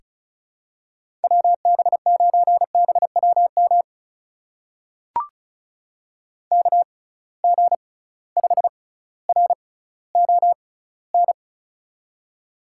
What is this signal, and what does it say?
Morse code, heard as a keyed tone, about 36 wpm, spaced as for 18 wpm, 700 Hz. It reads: WB9BWM KG5RON
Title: USA Call Signs - 6 Characters - 6x Character Spacing - 35wpm